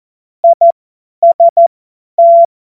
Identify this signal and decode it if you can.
Morse code, heard as a keyed tone, about 14 wpm, spaced as for 10 wpm, 680 Hz. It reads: IST
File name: Witz-12-August-2025-50-F70